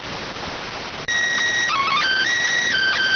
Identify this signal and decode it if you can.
SSTV (slow-scan television), PD 290.